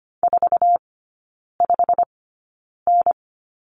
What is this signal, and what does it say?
Morse code, heard as a keyed tone, about 25 wpm, spaced as for 9 wpm, 700 Hz. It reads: 45N
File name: Lesson 17e - Letter u - Three Characters 25wpm